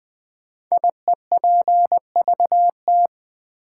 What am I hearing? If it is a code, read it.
Morse code, heard as a keyed tone, about 20 wpm, 700 Hz. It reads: IEPVT
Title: Code Groups - 5 Letters - 20wpm